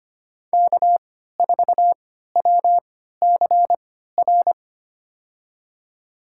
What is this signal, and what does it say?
Morse code, heard as a keyed tone, about 25 wpm, spaced as for 14 wpm, 700 Hz. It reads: K4WCR
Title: USA Call Signs - 5 Characters - 3x Character Spacing - 25wpm